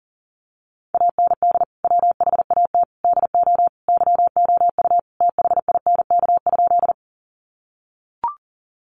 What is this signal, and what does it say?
Morse code, heard as a keyed tone, about 40 wpm, 700 Hz. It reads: AND WHAT DO YOU THINK?
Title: Sentences from Top 200 Words 40wpm